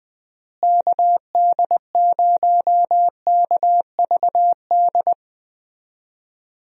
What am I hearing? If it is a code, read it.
Morse code, heard as a keyed tone, about 20 wpm, 700 Hz. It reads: KD0KVD
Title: USA Call Signs - 6 Characters - Repeated 2x - 20wpm